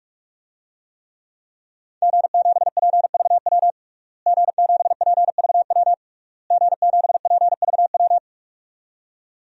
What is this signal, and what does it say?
Morse code, heard as a keyed tone, about 41 wpm, 700 Hz. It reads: G7PVW G7PVW G7PVW
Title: DXCC-100 Call Signs - 5 Characters - Repeated 3x - 45wpm